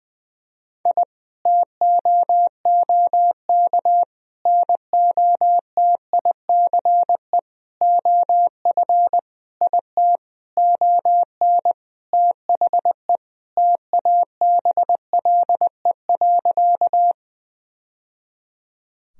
Morse code, heard as a keyed tone, about 20 wpm, 700 Hz. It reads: I TOOK NOTICE OF IT ON THE TABLE.
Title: Sentences from Top 1000 Words - Review Part 2 - 20wpm